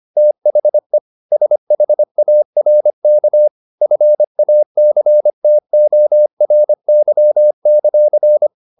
Morse code, heard as a keyed tone, about 25 wpm, 600 Hz. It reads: THE SHARK FACTORY;